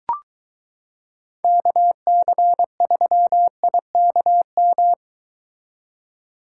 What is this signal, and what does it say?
Morse code, heard as a keyed tone, about 22 wpm, 700 Hz. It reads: KC3IKM